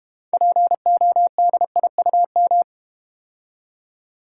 Morse code, heard as a keyed tone, about 32 wpm, 700 Hz. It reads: PODIUM